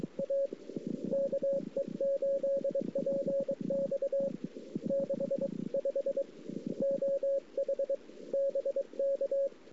Morse code, heard as a keyed tone, about 23 wpm, 550 Hz. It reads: A KE8PX 65 OH BK
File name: CW / 2008-ss-cw-k5tr-op-n5rz -00:00